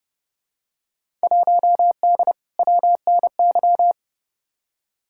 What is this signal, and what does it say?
Morse code, heard as a keyed tone, about 30 wpm, 700 Hz. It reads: 1D WNY
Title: Field Day Exchanges - 30wpm